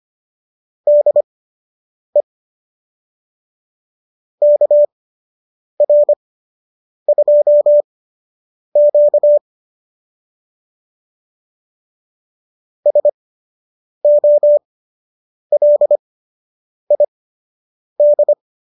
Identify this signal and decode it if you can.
Morse code, heard as a keyed tone, about 25 wpm, spaced as for 8 wpm, 590 Hz. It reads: DE KR2Q SOLID